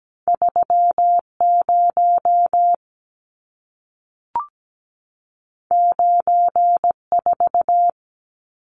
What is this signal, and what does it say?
Morse code, heard as a keyed tone, about 17 wpm, 700 Hz. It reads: 30 94
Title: Numbers - 2-Digit 17wpm